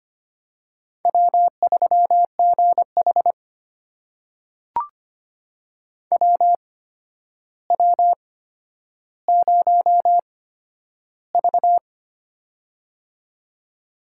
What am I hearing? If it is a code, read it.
Morse code, heard as a keyed tone, about 25 wpm, spaced as for 11 wpm, 700 Hz. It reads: W3GH WW0V
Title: USA Call Signs - 4 Characters - 8x Character Spacing - 25wpm